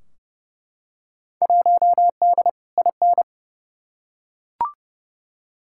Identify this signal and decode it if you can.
Morse code, heard as a keyed tone, about 30 wpm, 700 Hz. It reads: 1D IN
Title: Field Day Exchanges Encore - 30wpm